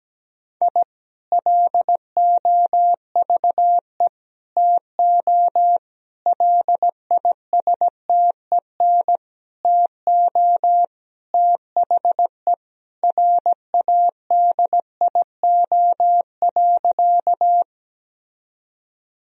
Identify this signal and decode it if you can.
Morse code, heard as a keyed tone, about 17 wpm, 700 Hz. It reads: I LOVE TO LISTEN TO THE RADIO.
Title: Sentences from Top 1000 Words - Review Part 3 - 17wpm